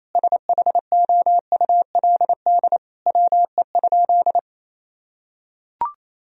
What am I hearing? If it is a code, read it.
Morse code, heard as a keyed tone, about 28 wpm, 700 Hz. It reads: SHOULD WE?
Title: Sentences from Top 1000 Words - Review Part 3 - 28wpm